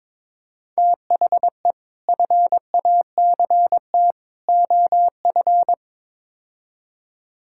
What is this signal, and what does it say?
Morse code, heard as a keyed tone, about 22 wpm, 700 Hz. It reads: THE FACT OF
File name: practice / 22wpm / Build A Phrase - 5-Word Phrases - 22wpm